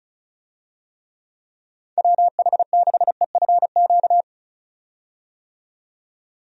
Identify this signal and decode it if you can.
Morse code, heard as a keyed tone, about 36 wpm, 700 Hz. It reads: WH6EFQ
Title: USA Call Signs - 6 Characters - 35wpm